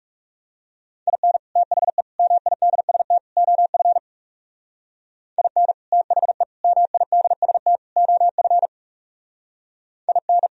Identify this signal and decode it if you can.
Morse code, heard as a keyed tone, about 35 wpm, 700 Hz. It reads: IN THE MIDST OF IN THE MIDST OF IN
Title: Speed-Racing - Sets of 4 Words 30wpm